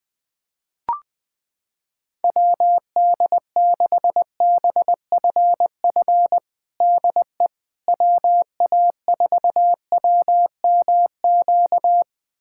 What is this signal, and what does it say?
Morse code, heard as a keyed tone, about 20 wpm, 700 Hz. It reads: WD6BFF DE WA4WMQ
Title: USA Callsign DE Callsign - 6 Characters - 20wpm